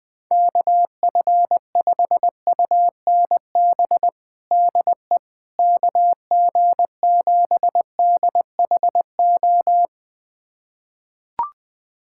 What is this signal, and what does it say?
Morse code, heard as a keyed tone, about 20 wpm, 700 Hz. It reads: KF5UNB DE KG7DHO
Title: USA Callsign DE Callsign - 6 Characters - 20wpm